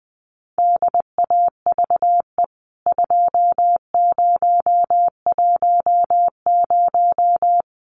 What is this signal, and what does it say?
Morse code, heard as a keyed tone, about 20 wpm, 700 Hz. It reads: DAVE 2010
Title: CWT Exchange 20wpm